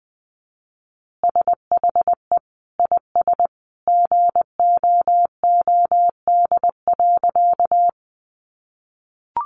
Morse code, heard as a keyed tone, about 20 wpm, 700 Hz. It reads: SHE IS GOOD.
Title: Easy Phrases 20wpm